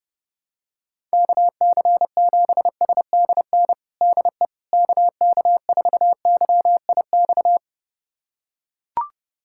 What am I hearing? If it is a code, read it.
Morse code, heard as a keyed tone, about 32 wpm, 700 Hz. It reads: KC7SDN DE KK4YIX
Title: USA Callsign DE Callsign - 6 Characters - 30wpm